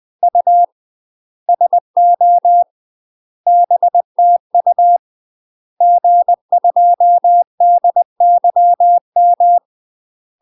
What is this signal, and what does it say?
Morse code, heard as a keyed tone, about 20 wpm, 700 Hz. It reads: U SO BTU G2DYM
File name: UK MASTER QSOs TRAINING / 20WPM / UK MASTER QSOs 0030